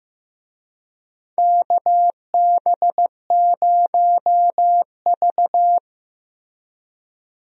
Morse code, heard as a keyed tone, about 15 wpm, 700 Hz. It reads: KB0V
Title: USA Call Signs - 4 Characters - Repeated 3x - 15wpm